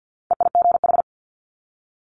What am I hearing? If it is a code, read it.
Morse code, heard as a keyed tone, about 50 wpm, 700 Hz. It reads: EIGH
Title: Multi-letter Phonemes 50wpm